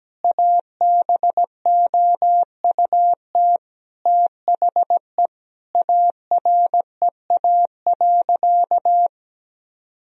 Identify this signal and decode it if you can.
Morse code, heard as a keyed tone, about 17 wpm, 700 Hz. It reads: ABOUT THE AREA.
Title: Sentences from Top 1000 Words - Review Part 2 - 17wpm